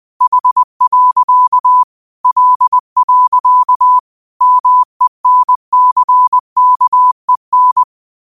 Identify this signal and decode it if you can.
Morse code, heard as a keyed tone, about 20 wpm, 1000 Hz. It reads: H. L. MENCKEN
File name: 20wpm